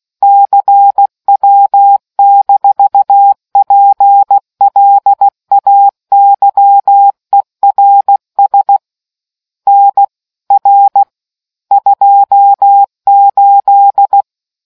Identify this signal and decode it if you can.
Morse code, heard as a keyed tone, about 16 wpm, 800 Hz. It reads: CW-PLAYERS NR 28